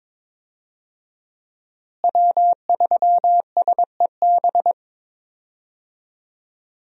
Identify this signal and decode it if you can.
Morse code, heard as a keyed tone, about 21 wpm, 700 Hz. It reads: W3SEB
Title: USA Call Signs - 5 Characters - Repeated 2x - 22wpm